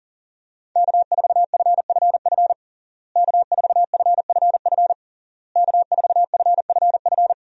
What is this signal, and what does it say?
Morse code, heard as a keyed tone, about 40 wpm, 700 Hz. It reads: K4FFF K4FFF K4FFF
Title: USA Call Signs - 5 Characters - Repeated 3x - 40wpm